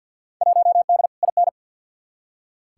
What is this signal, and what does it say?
Morse code, heard as a keyed tone, about 47 wpm, 700 Hz. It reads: 1D IN
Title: Field Day Exchanges - 50wpm